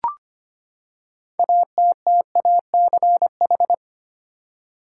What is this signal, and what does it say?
Morse code, heard as a keyed tone, about 25 wpm, 700 Hz. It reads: ATTACH
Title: Six Letter Words - 25wpm